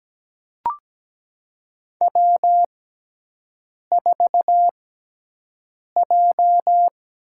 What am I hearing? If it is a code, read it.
Morse code, heard as a keyed tone, about 17 wpm, spaced as for 6 wpm, 700 Hz. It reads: W4J
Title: USA Call Signs - 4 Characters - 6x Character Spacing - 17wpm